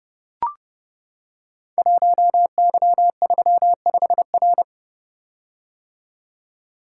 Morse code, heard as a keyed tone, about 30 wpm, 700 Hz. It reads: 1Y35R
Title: Code Groups - 5 Letter:Numbers - 30wpm